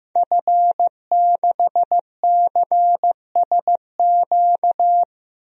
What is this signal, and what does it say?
Morse code, heard as a keyed tone, about 15 wpm, 700 Hz. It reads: F6CSQ
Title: DXCC-100 Call Signs - 5 Characters - 15wpm